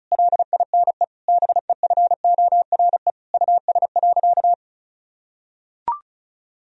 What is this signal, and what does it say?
Morse code, heard as a keyed tone, about 36 wpm, 700 Hz. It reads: LINE BEFORE US.
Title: Sentences from Top 1000 Words - Review Part 1 - 35wpm